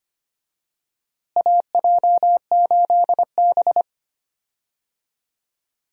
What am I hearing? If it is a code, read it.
Morse code, heard as a keyed tone, about 25 wpm, 700 Hz. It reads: AJ8B
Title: USA Call Signs - 4 Characters - Repeated 2x - 25wpm